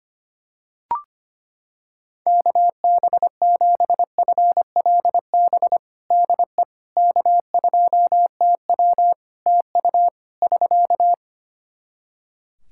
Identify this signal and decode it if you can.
Morse code, heard as a keyed tone, about 25 wpm, 700 Hz. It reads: KB7FLB DE K2TW TU <SK>
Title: Practice QSO - Exchange 4 - Ending - 25wpm